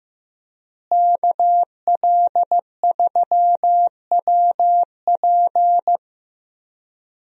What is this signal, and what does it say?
Morse code, heard as a keyed tone, about 16 wpm, 700 Hz. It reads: KL3WP
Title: USA Call Signs - 5 Characters - Repeated 2x - 15wpm